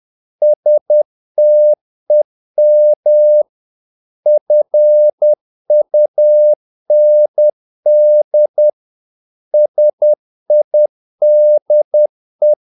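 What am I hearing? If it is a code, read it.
Morse code, heard as a keyed tone, about 10 wpm, 600 Hz. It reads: STEM FUND SIDE
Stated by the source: Morse practice of short Danish words at 10 wpm, from CW med Gnister.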